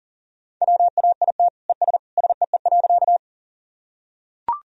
Morse code, heard as a keyed tone, about 40 wpm, 700 Hz. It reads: WAIT ES SEE.
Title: Sentences from Top 500 Words and Abbreviations 40wpm